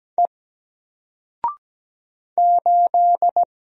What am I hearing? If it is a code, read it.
Morse code, heard as a keyed tone, about 17 wpm, 700 Hz. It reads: E 8
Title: Single Letter-Number 17wpm